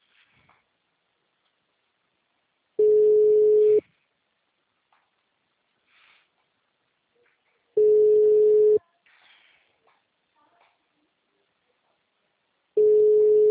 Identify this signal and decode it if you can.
Ringback tone (Europe, 425 Hz), with speech.